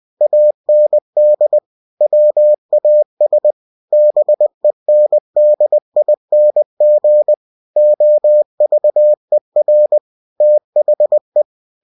Morse code, heard as a keyed tone, about 20 wpm, 600 Hz. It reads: AND WAS BENDING OVER THE
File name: cw-books / winnie-the-pooh / winnie-20wpm-page-0022